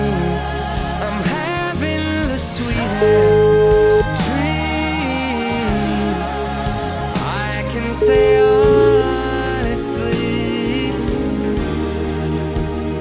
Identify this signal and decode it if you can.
Ringback tone (Europe, 425 Hz), with noise.